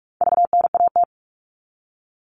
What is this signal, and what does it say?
Morse code, heard as a keyed tone, about 44 wpm, 700 Hz. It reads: VNAT
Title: All Cut Numbers - 4-Digit 45wpm